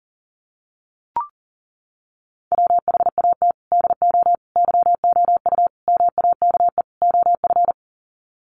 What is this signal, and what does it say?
Morse code, heard as a keyed tone, about 40 wpm, 700 Hz. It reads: WHAT DO YOU MAKE OF
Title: Sets of 5 Words 40wpm